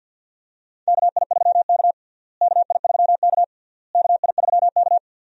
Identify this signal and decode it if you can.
Morse code, heard as a keyed tone, about 53 wpm, 700 Hz. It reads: KI3K KI3K KI3K
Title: USA Call Signs - 4 Characters - Repeated 3x - 50wpm